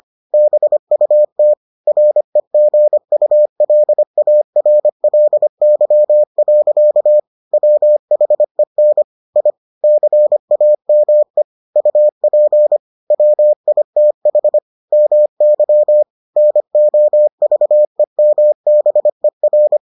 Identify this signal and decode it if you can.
Morse code, heard as a keyed tone, about 25 wpm, 600 Hz. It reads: BUT REGULARLY. WHEN I CAME UP WITH MY NOVEMBER